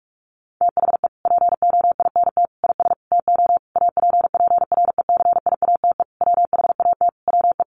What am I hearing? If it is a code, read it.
Morse code, heard as a keyed tone, about 44 wpm, 700 Hz. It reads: THE POINT IS TO APPRECIATE WHAT WE
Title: Sentences from Top 1300 Words 45wpm